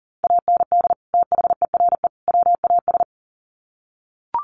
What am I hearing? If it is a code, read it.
Morse code, heard as a keyed tone, about 40 wpm, 700 Hz. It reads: AND THERE WAS
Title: Sets of 3 Words 40wpm